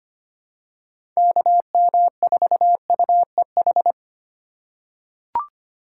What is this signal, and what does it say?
Morse code, heard as a keyed tone, about 25 wpm, 700 Hz. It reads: KM4UEH
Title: USA Call Signs - 6 Characters - 2x Character Spacing - 25wpm